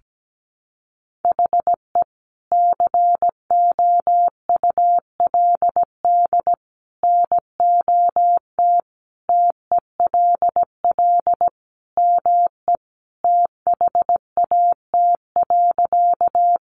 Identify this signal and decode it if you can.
Morse code, heard as a keyed tone, about 17 wpm, 700 Hz. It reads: HE COULD NOT TELL ME THAT.
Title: Sentences from Top 600 Words 17wpm